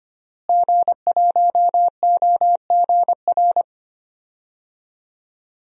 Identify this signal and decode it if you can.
Morse code, heard as a keyed tone, about 25 wpm, 700 Hz. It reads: G1OGR
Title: DXCC-100 Call Signs - 5 Characters - Repeated 2x - 25wpm